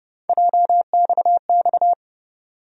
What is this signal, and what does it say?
Morse code, heard as a keyed tone, about 30 wpm, 700 Hz. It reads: JXX
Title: USA Call Sign Postfixes - 3 Letters 30wpm